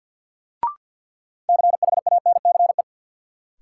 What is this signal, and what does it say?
Morse code, heard as a keyed tone, about 53 wpm, 700 Hz. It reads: CHANCE
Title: Sentences from Top 800 Words - New Word Drill - 50wpm